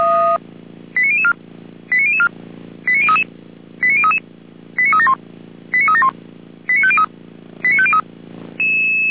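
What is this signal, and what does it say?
Five-tone selective call (ZVEI), heard as quick runs of five tones, tones of about 70 ms. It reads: 89004 89004 89033 89033 89371 89371 89503 89503
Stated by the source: ZVEI five-tone selective call code 89001, received by SDR, from fire brigade Feuerwehr Orlamünde's audio server.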